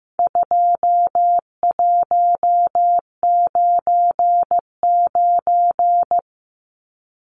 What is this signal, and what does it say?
Morse code, heard as a keyed tone, about 15 wpm, 700 Hz. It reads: 2199
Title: Numbers - 4-Digit 15wpm